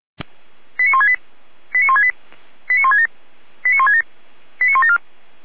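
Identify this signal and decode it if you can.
Five-tone selective call (ZVEI), heard as quick runs of five tones, tones of about 80 ms. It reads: 89168 89168 89167 89167 89174
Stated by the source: ZVEI five-tone selective call 89158, received by SDR.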